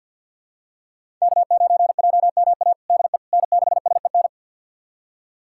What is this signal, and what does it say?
Morse code, heard as a keyed tone, about 53 wpm, 700 Hz. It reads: K9JMA DE N6SEN
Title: USA Callsign DE Callsign - 5 Characters - 50wpm